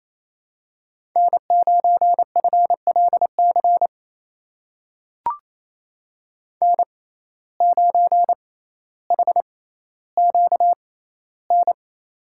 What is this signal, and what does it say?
Morse code, heard as a keyed tone, about 28 wpm, spaced as for 14 wpm, 700 Hz. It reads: N9FLC N9HQN
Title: USA Call Signs - 5 Characters - 6x Character Spacing - 28wpm